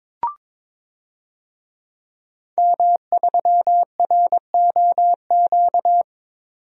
Morse code, heard as a keyed tone, about 22 wpm, 700 Hz. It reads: M3ROQ
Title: DXCC-100 Call Signs - 5 Characters - 22wpm